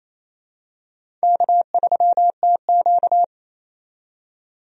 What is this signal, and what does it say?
Morse code, heard as a keyed tone, about 28 wpm, 700 Hz. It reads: K3TQ
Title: USA Call Signs - 4 Characters - 28wpm